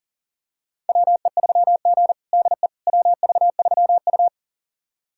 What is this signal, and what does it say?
Morse code, heard as a keyed tone, about 40 wpm, 700 Hz. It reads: WE3G DE WV3U
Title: USA Callsign DE Callsign - 4 Characters - 40wpm